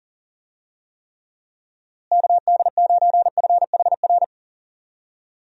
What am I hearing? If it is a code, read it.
Morse code, heard as a keyed tone, about 40 wpm, 700 Hz. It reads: KD9FHR
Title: USA Call Signs - 6 Characters - Repeated 2x - 40wpm